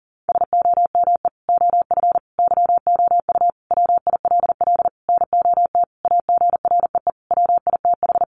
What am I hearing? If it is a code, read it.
Morse code, heard as a keyed tone, about 40 wpm, 700 Hz. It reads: SOME OF YOU WILL NOT AGREE WITH
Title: Sentences from Top 1100 Words 40wpm